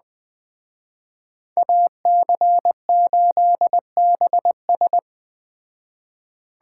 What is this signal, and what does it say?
Morse code, heard as a keyed tone, about 20 wpm, 700 Hz. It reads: AC8BS